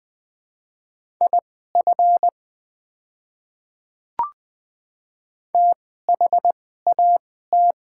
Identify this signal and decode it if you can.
Morse code, heard as a keyed tone, about 21 wpm, spaced as for 15 wpm, 700 Hz. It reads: IF THAT
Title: 2x Character Spacing - Top 100 Words - 20wpm